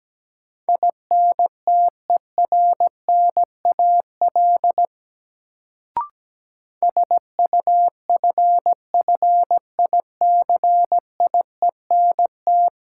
Morse code, heard as a keyed tone, about 17 wpm, 700 Hz. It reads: INTERNAL SUFFICIENT
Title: Sentences from Top 1400 Words - New Word Drill Alt - 17wpm